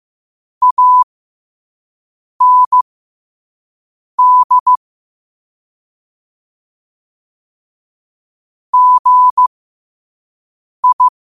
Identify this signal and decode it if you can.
Morse code, heard as a keyed tone, about 15 wpm, spaced as for 5 wpm, 1000 Hz. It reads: AND GI
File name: qotdfiles / 5wpm